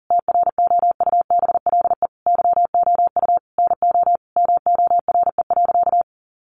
Morse code, heard as a keyed tone, about 37 wpm, 700 Hz. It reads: TROUBLE YOU NO MORE.
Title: Sentences from Top 1200 Words 40wpm